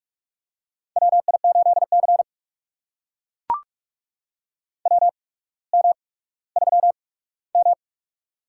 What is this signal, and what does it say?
Morse code, heard as a keyed tone, about 44 wpm, spaced as for 19 wpm, 700 Hz. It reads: WI8C WM3M
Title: USA Call Signs - 4 Characters - 8x Character Spacing - 45wpm